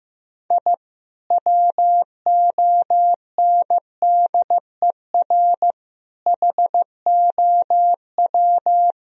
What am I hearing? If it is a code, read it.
Morse code, heard as a keyed tone, about 15 wpm, 700 Hz. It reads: I WONDER HOW
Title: Sentences from Top 1000 Words - Review Part 3 - 15wpm